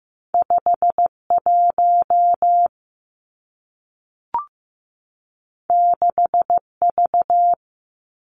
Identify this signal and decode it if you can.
Morse code, heard as a keyed tone, about 15 wpm, 700 Hz. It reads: 51 6V
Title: Two Challenging Letter-Number Combinations 15wpm